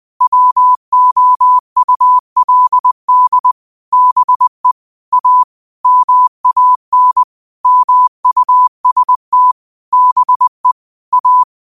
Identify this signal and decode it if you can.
Morse code, heard as a keyed tone, about 20 wpm, 1000 Hz. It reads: WOULD BE A MAN MUST BE A
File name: qotdfiles / 20wpm